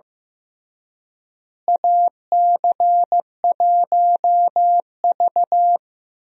Morse code, heard as a keyed tone, about 15 wpm, 700 Hz. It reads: AC1V
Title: USA Call Signs - 4 Characters - Repeated 3x - 15wpm